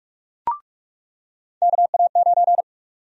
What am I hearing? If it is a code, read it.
Morse code, heard as a keyed tone, about 47 wpm, 700 Hz. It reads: KA9
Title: USA Call Sign Prefixes - 45wpm